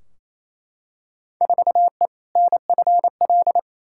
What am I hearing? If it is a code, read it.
Morse code, heard as a keyed tone, about 28 wpm, 700 Hz. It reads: 4E NFL
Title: Field Day Exchanges Encore - 28wpm